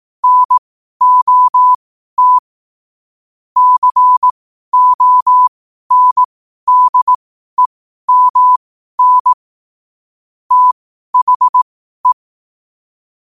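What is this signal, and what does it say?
Morse code, heard as a keyed tone, about 18 wpm, spaced as for 12 wpm, 1000 Hz. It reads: NOT CONDEMN THE
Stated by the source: Morse practice sent at 12 wpm.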